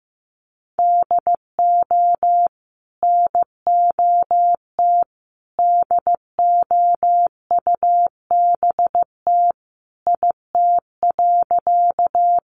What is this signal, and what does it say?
Morse code, heard as a keyed tone, about 14 wpm, 700 Hz. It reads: DO NOT DOUBT IT.
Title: Sentences from Top 1500 Words 15wpm